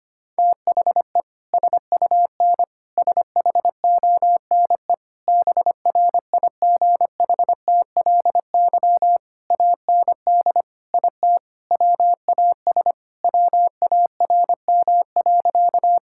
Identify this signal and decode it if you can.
Morse code, heard as a keyed tone, about 25 wpm, 700 Hz. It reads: THE SUN SHONE BRIGHTLY AND IT WAS WARM.
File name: practice / 25wpm / Sentences with 2-Word Phrases - 25wpm